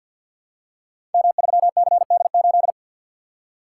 Morse code, heard as a keyed tone, about 46 wpm, 700 Hz. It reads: M3CD8
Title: Code Groups - 5 Letter:Numbers - 50wpm